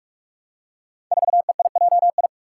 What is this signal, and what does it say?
Morse code, heard as a keyed tone, about 44 wpm, 700 Hz. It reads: 4EIJI
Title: Five Challenging Letter-Number Combinations 45wpm